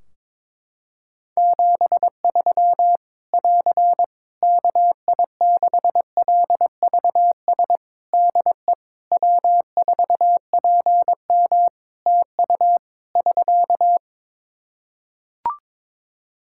Morse code, heard as a keyed tone, about 22 wpm, 700 Hz. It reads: 73 <AR> KI6LVS DE W4PM TU <SK>
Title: Practice QSO - Exchange 4 - Ending - 22wpm